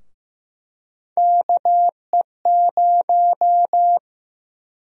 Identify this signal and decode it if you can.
Morse code, heard as a keyed tone, about 15 wpm, 700 Hz. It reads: KE0
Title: USA Call Sign Prefixes - Repeated 2x - 15wpm